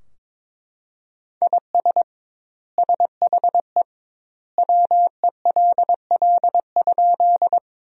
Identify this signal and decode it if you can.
Morse code, heard as a keyed tone, about 22 wpm, 700 Hz. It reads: IS SHE WELL?
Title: 2x Word-Spacing - Sentences from Top 200 Words - 22wpm